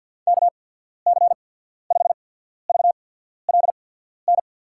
Morse code, heard as a keyed tone, about 53 wpm, 700 Hz. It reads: K C 5 V L N
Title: USA Call Signs - 6 Characters - 8x Character Spacing - 50wpm